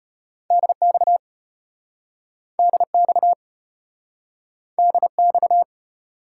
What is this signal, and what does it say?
Morse code, heard as a keyed tone, about 34 wpm, 700 Hz. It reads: DX DX DX